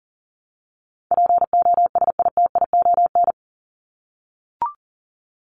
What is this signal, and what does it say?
Morse code, heard as a keyed tone, about 40 wpm, 700 Hz. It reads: POSITION
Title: Top 751-1000 Words 40wpm